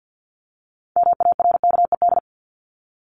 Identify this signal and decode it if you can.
Morse code, heard as a keyed tone, about 50 wpm, 700 Hz. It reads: MARKED
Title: ED Suffix - Top 100 Words with Base Words 50wpm